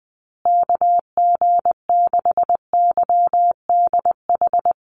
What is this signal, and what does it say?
Morse code, heard as a keyed tone, about 20 wpm, 700 Hz. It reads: KG6YDH
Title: USA Call Signs Repeated 3x 20wpm